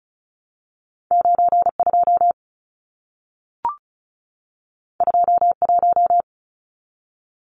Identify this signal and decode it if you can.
Morse code, heard as a keyed tone, about 33 wpm, 700 Hz. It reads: 92 21
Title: Numbers - 2-Digit 35wpm